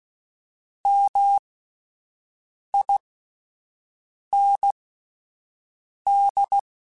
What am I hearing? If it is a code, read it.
Morse code, heard as a keyed tone, about 16 wpm, spaced as for 5 wpm, 800 Hz. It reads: MIND